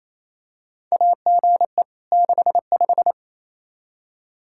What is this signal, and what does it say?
Morse code, heard as a keyed tone, about 28 wpm, 700 Hz. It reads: AGE 65
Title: K1USN SST QSO Practice 28wpm